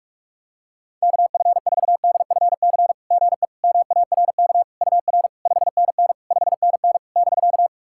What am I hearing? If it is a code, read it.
Morse code, heard as a keyed tone, about 44 wpm, 690 Hz. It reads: KF4DFC GE MARK UR 5NN 5NN <BK>
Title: SOTA QSO Practice 45wpm